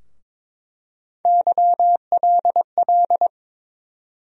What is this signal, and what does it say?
Morse code, heard as a keyed tone, about 22 wpm, 700 Hz. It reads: YLL
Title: USA Call Sign Postfixes - 3 Letters 22wpm